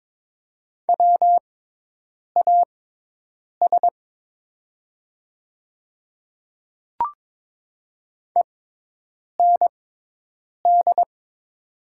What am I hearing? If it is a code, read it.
Morse code, heard as a keyed tone, about 22 wpm, spaced as for 8 wpm, 700 Hz. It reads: WAS END
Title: 6x Character Spacing - Top 200 Words - 22wpm